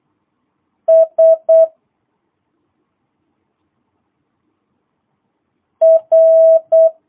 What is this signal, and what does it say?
Morse code, heard as a keyed tone, about 8 wpm, 650 Hz. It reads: S R